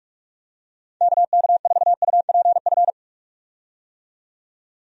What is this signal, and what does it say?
Morse code, heard as a keyed tone, about 45 wpm, 700 Hz. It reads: KK4UPF
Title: USA Call Signs - 6 Characters - 45wpm